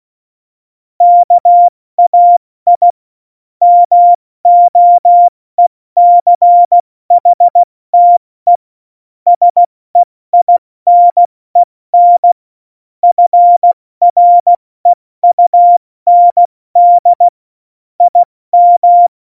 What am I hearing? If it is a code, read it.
Morse code, heard as a keyed tone, about 16 wpm, spaced as for 14 wpm, 700 Hz. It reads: KAI MOECHTE SEINEN FREUND IM
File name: Morsekurs / Witze / Witz 20. Oktober 2020, 70 F80